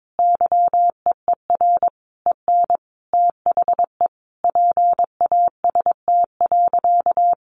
Morse code, heard as a keyed tone, about 22 wpm, 700 Hz. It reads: YEER EN THE PAST.
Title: A Sound Of Thunder 22wpm